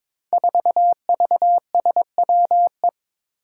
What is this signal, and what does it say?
Morse code, heard as a keyed tone, about 22 wpm, 700 Hz. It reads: 4VSWE
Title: Five Challenging Letter-Number Combinations 22wpm